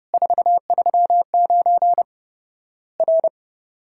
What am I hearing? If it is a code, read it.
Morse code, heard as a keyed tone, about 30 wpm, 700 Hz. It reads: 439 R
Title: Worked All Europe DX Contest - QTC Exchange - 30wpm